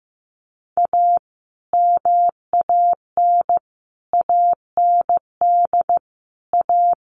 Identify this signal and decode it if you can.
Morse code, heard as a keyed tone, about 15 wpm, 700 Hz. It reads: A MAN AND A
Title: Sets of 5 Words Encore 15wpm